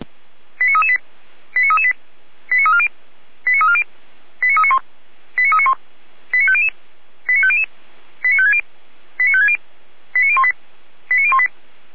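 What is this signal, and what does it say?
Five-tone selective call (ZVEI), heard as quick runs of five tones, tones of about 70 ms. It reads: 89338 89338 89340 89340 89391 89391 89500 89500 89560 89560 89018 89018
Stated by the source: ZVEI five-tone call 89337 received on an SDR by Feuerwehr Orlamünde.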